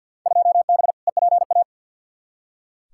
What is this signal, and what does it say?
Morse code, heard as a keyed tone, about 45 wpm, 700 Hz. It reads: 2B EPA
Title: Field Day Exchanges Encore - 50wpm